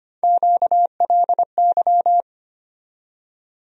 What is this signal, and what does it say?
Morse code, heard as a keyed tone, about 25 wpm, 700 Hz. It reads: QLY
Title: USA Call Sign Postfixes - 3 Letters 25wpm